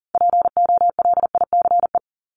Morse code, heard as a keyed tone, about 40 wpm, 700 Hz. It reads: POLICE
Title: Top 1751-2000 Words 40wpm